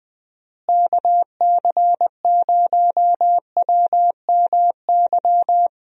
Morse code, heard as a keyed tone, about 20 wpm, 700 Hz. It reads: KC0WMY
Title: USA Call Signs - 6 Characters - Repeated 2x - 20wpm